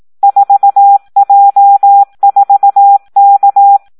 Morse code, heard as a keyed tone, about 18 wpm, 800 Hz. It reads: 4J4K
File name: Audio practica CW / SD 1 18